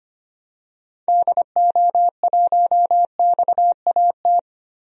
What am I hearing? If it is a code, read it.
Morse code, heard as a keyed tone, about 25 wpm, 700 Hz. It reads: DO1XAT
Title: DXCC-100 Call Signs - 6 Characters - Repeated 2x - 25wpm